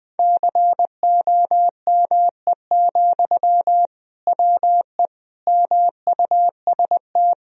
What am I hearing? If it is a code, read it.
Morse code, heard as a keyed tone, about 21 wpm, 700 Hz. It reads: COME, WE MUST
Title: Sentences from Top 1000 Words - Review Part 1 - 20wpm